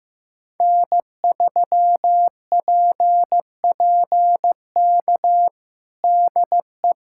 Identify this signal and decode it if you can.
Morse code, heard as a keyed tone, about 15 wpm, 700 Hz. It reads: N3PPK DE
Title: USA Callsign DE Callsign - 5 Characters - 15wpm